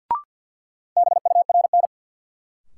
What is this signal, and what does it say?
Morse code, heard as a keyed tone, about 48 wpm, 700 Hz. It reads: BURN
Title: Sentences from Top 1300 Words - New Word Drill Alt - 50wpm